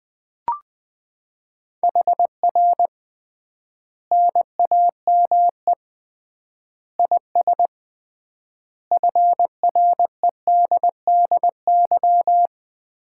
Morse code, heard as a keyed tone, about 20 wpm, 700 Hz. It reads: HR NAME IS FREDDY
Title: Common QSO Phrases - 3x Word-Spacing - 20wpm